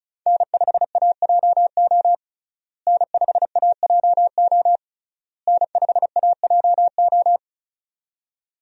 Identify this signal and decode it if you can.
Morse code, heard as a keyed tone, about 36 wpm, 700 Hz. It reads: N5AJO N5AJO N5AJO
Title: USA Call Signs - 5 Characters - Repeated 3x - 35wpm